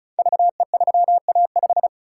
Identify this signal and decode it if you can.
Morse code, heard as a keyed tone, about 36 wpm, 690 Hz. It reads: VE3A5
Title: Code Groups - 5 Letter:Numbers - 35wpm